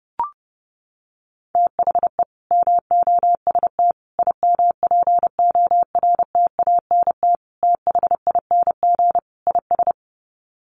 Morse code, heard as a keyed tone, about 28 wpm, 700 Hz. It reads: THE MOST IMPORTANT THING IS
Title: Sets of 5 Words Encore 30wpm